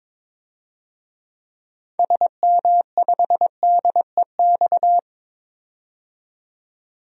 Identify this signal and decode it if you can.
Morse code, heard as a keyed tone, about 23 wpm, 700 Hz. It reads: SM5DEX
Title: DXCC-100 Call Signs - 6 Characters - 22wpm